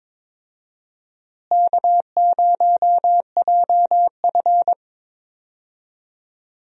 Morse code, heard as a keyed tone, about 22 wpm, 700 Hz. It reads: K0JF
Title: USA Call Signs - 4 Characters - Repeated 2x - 22wpm